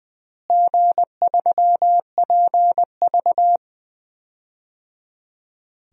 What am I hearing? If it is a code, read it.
Morse code, heard as a keyed tone, about 21 wpm, 700 Hz. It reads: G3PV
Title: DXCC-100 Call Signs - 4 Characters - Repeated 3x - 20wpm